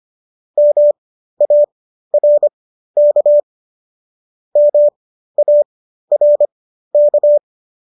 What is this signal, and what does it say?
Morse code, heard as a keyed tone, about 25 wpm, spaced as for 13 wpm, 590 Hz. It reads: MARK MARK